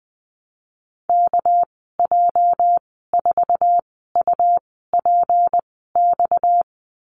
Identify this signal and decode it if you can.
Morse code, heard as a keyed tone, about 20 wpm, spaced as for 15 wpm, 700 Hz. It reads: KJ4UPX
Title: 238 Lesson 38g - Letter q - Call Signs 20wpm